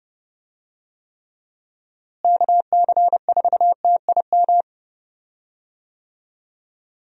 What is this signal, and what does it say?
Morse code, heard as a keyed tone, about 32 wpm, 700 Hz. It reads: KC4TIM
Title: USA Call Signs - 6 Characters - Repeated 3x - 30wpm